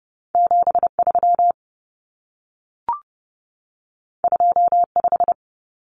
Morse code, heard as a keyed tone, about 30 wpm, 700 Hz. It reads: 73 25
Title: Numbers - 2-Digit 30wpm